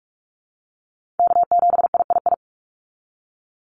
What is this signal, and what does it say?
Morse code, heard as a keyed tone, about 46 wpm, 690 Hz. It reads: K7III
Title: NA SOTA Chaser Callsigns 45wpm